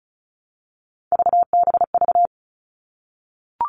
Morse code, heard as a keyed tone, about 34 wpm, 700 Hz. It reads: VBV
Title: All Cut Numbers - 3-Digit 35wpm